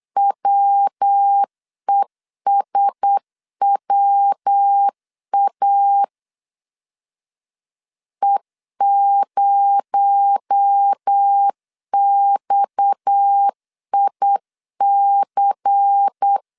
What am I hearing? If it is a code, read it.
Morse code, heard as a keyed tone, about 8 wpm, 800 Hz. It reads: WESWA E0XIC
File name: CW / DET / 1 6 1500C S B 8wpm 5142